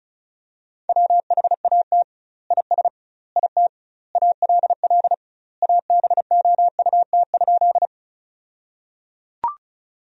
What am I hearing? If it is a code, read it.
Morse code, heard as a keyed tone, about 34 wpm, 700 Hz. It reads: WHAT IS IT ALL ABOUT?
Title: 2x Word-Spacing - Sentences from Top 100 Words - 35wpm